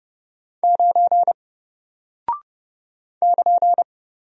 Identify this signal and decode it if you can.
Morse code, heard as a keyed tone, about 30 wpm, 700 Hz. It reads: 9 <KN>
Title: Single Character (Letter:Number:Punctuation:Prosign) Alt 30wpm